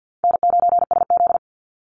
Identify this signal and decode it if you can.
Morse code, heard as a keyed tone, about 50 wpm, 700 Hz. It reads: N9SZ
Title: Easy Call Signs Repeated 2x - 50wpm